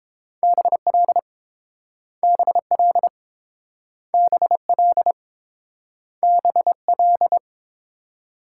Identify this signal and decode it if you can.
Morse code, heard as a keyed tone, about 30 wpm, 700 Hz. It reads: BL BL BL BL